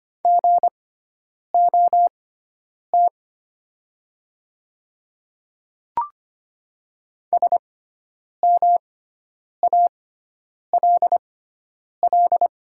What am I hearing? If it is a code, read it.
Morse code, heard as a keyed tone, about 25 wpm, spaced as for 9 wpm, 700 Hz. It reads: GOT SMALL
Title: 6x Character Spacing - Top 500 Words - 25wpm